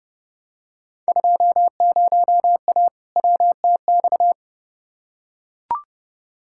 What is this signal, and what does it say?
Morse code, heard as a keyed tone, about 31 wpm, 700 Hz. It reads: 20A WTX
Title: Field Day Exchanges Encore - 30wpm